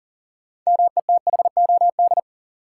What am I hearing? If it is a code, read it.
Morse code, heard as a keyed tone, about 40 wpm, 700 Hz. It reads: METHOD
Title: Sentences from Top 600 Words - New Word Drill - 40wpm